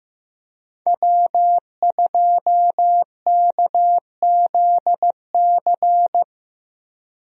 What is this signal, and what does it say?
Morse code, heard as a keyed tone, about 14 wpm, 700 Hz. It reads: W2KZC
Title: USA Call Signs - 5 Characters - Repeated 3x - 15wpm